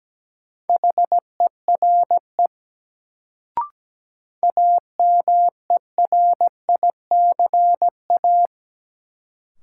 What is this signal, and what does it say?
Morse code, heard as a keyed tone, about 17 wpm, 700 Hz. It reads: HERE AMERICA
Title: Sets of 5 Words - New Word Drill - 17wpm